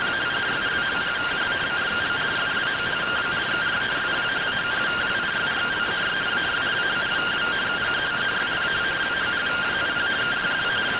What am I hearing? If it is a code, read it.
Two-tone teleprinter signal, about 170 Hz shift.